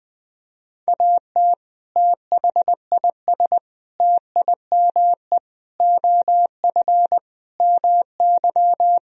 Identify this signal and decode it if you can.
Morse code, heard as a keyed tone, about 20 wpm, 700 Hz. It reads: AT THIS TIME OF MY
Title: Sentences from Top 300 Words and Abbreviations 20wpm